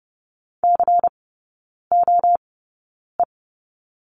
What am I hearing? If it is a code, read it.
Morse code, heard as a keyed tone, about 30 wpm, spaced as for 9 wpm, 700 Hz. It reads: COE
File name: Lesson 16e - Letter c - Three Characters 30wpm